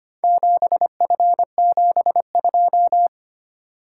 Morse code, heard as a keyed tone, about 25 wpm, 700 Hz. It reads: 7F72
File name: practice / 25wpm / Four Letter-Number Combinations - 25wpm